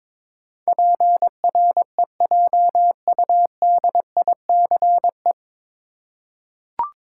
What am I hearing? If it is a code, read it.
Morse code, heard as a keyed tone, about 22 wpm, 700 Hz. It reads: PREJUDICE
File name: 9 Letter Words - First 3 Letters Given - 22wpm